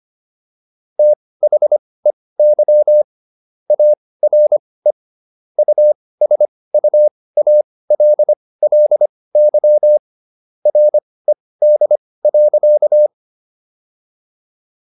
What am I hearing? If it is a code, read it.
Morse code, heard as a keyed tone, about 25 wpm, spaced as for 18 wpm, 600 Hz. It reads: THEY ARE USUALLY RED.